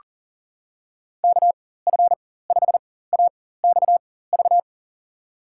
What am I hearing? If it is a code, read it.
Morse code, heard as a keyed tone, about 40 wpm, spaced as for 19 wpm, 700 Hz. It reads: KF5AXV
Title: USA Call Signs - 6 Characters - 4x Character Spacing - 40wpm